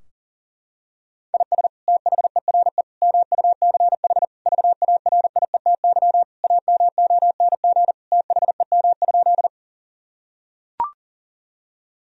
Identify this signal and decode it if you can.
Morse code, heard as a keyed tone, about 40 wpm, 700 Hz. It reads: IS THERE MUCH VARIETY AMONG THEM?
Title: Sentences from Top 1000 Words - Review Part 4 - 40wpm